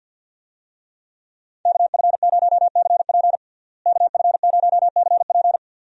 Morse code, heard as a keyed tone, about 47 wpm, 700 Hz. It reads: KF0CP KF0CP
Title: USA Call Signs - 5 Characters - Repeated 3x - 50wpm